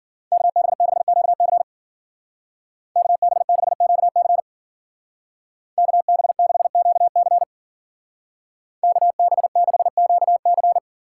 Morse code, heard as a keyed tone, about 50 wpm, 700 Hz. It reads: KB6QC KB6QC KB6QC KB6QC
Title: Speed Racing - USA 2x2 Call Signs 40wpm